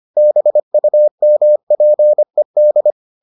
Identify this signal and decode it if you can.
Morse code, heard as a keyed tone, about 25 wpm, 600 Hz. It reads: BUMPED